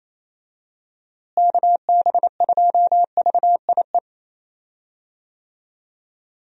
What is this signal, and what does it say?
Morse code, heard as a keyed tone, about 28 wpm, 700 Hz. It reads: KB2VIE
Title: USA Call Signs - 6 Characters - Repeated 2x - 28wpm